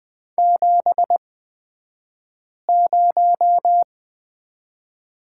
Morse code, heard as a keyed tone, about 20 wpm, spaced as for 5 wpm, 700 Hz. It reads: 70
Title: 223 Lesson 36d - BT - Two Characters 20wpm